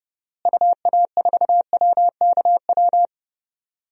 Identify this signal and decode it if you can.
Morse code, heard as a keyed tone, about 30 wpm, 700 Hz. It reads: UA4WKW
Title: DXCC-100 Call Signs - 6 Characters - Repeated 3x - 30wpm